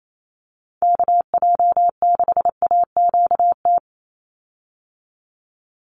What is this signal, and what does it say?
Morse code, heard as a keyed tone, about 28 wpm, 700 Hz. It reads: KJ6AQT
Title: USA Call Signs 28wpm